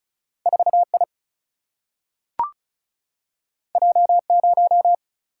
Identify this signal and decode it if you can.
Morse code, heard as a keyed tone, about 36 wpm, 700 Hz. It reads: 4 I J 0
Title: Single-Letter Single-Number 35wpm